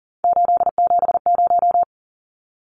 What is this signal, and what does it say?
Morse code, heard as a keyed tone, about 40 wpm, 700 Hz. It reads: 870
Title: Numbers - 3-Digit 40wpm